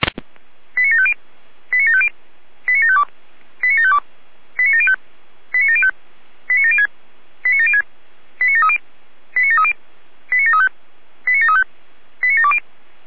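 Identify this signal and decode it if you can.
Five-tone selective call (ZVEI), heard as quick runs of five tones, tones of about 70 ms. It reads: 89750 89750 89752 89752 89705 89705 89706 89706 89730 89730 89735 89735 89720
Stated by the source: ZVEI five-tone selective call received on an SDR.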